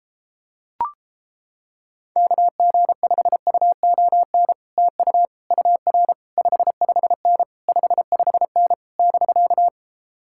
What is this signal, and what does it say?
Morse code, heard as a keyed tone, about 31 wpm, 700 Hz. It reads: KG5UON TU UR 55N 55N <BK>